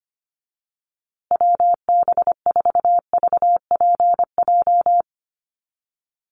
Morse code, heard as a keyed tone, about 25 wpm, 700 Hz. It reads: WB4VPJ
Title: USA Call Signs Repeated 2x 25wpm